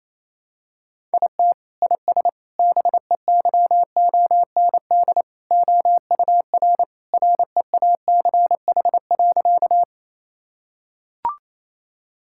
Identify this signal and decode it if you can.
Morse code, heard as a keyed tone, about 28 wpm, 700 Hz. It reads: IT IS BEYOND OUR REACH.
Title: Sentences from Top 1000 Words - Review Part 6 - 28wpm